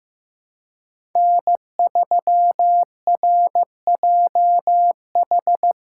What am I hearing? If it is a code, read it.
Morse code, heard as a keyed tone, about 15 wpm, 700 Hz. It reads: N3RJH
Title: USA Call Signs - 5 Characters - 3x Character Spacing - 15wpm